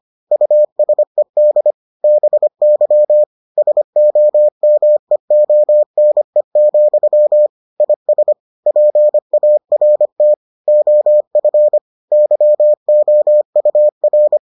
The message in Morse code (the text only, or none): USED BY SOMEONE, IS PART OF YOUR